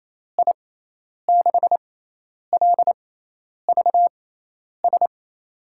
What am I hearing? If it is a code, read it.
Morse code, heard as a keyed tone, about 28 wpm, spaced as for 10 wpm, 700 Hz. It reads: I6LVS
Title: USA Call Signs - 6x Character Spacing - 28wpm